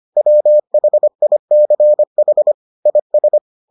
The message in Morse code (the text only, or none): WHICH IS